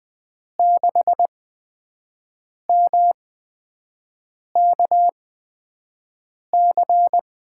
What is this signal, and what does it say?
Morse code, heard as a keyed tone, about 20 wpm, spaced as for 5 wpm, 700 Hz. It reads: 6MKC
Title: USA Call Signs - 6 Characters - 8x Character Spacing - 20wpm